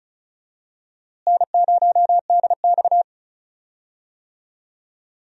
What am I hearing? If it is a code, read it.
Morse code, heard as a keyed tone, about 34 wpm, 700 Hz. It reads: N0DX
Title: USA Call Signs - 4 Characters - Repeated 2x - 35wpm